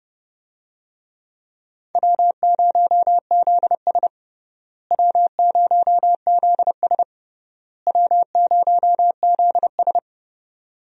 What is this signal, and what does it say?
Morse code, heard as a keyed tone, about 30 wpm, 700 Hz. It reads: W0ZS W0ZS W0ZS
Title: USA Call Signs - 4 Characters - Repeated 3x - 30wpm